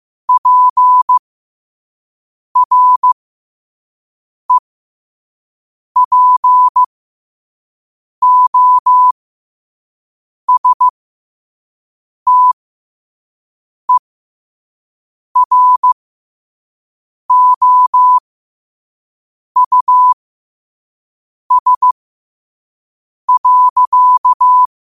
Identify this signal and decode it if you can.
Morse code, heard as a keyed tone, about 15 wpm, spaced as for 5 wpm, 1000 Hz. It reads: PREPOSTEROUS.
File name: qotdfiles / 5wpm